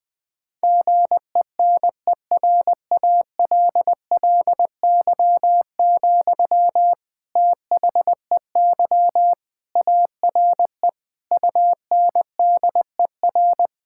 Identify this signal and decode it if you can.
Morse code, heard as a keyed tone, about 20 wpm, 700 Hz. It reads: GENERALLY, THEY ARE UNDER
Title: Sentences from Top 1000 Words - Review Part 6 - 20wpm